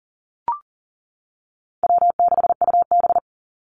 Morse code, heard as a keyed tone, about 40 wpm, 700 Hz. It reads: W6UB
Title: NA SOTA Activator Callsigns 40wpm